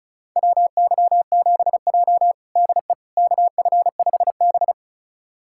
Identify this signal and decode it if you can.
Morse code, heard as a keyed tone, about 36 wpm, 700 Hz. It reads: WY7J DE KF5B
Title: USA Callsign DE Callsign - 4 Characters - 35wpm